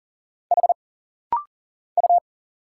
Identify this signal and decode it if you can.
Morse code, heard as a keyed tone, about 39 wpm, 700 Hz. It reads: H U
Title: Single Letter-Number Alt - Rapid-Fire 40wpm